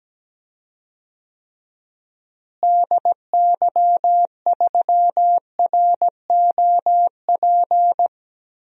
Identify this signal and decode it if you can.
Morse code, heard as a keyed tone, about 17 wpm, 700 Hz. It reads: DY3ROP
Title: DXCC-100 Call Signs - 6 Characters - Repeated 2x - 17wpm